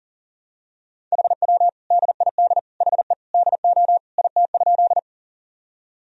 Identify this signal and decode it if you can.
Morse code, heard as a keyed tone, about 40 wpm, 700 Hz. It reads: HW DID HE DO IT?
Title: Sentences from Top 100 Words and Abbreviations 40wpm